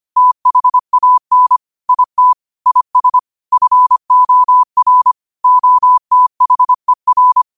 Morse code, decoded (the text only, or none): THAN IT IS FOR OTHER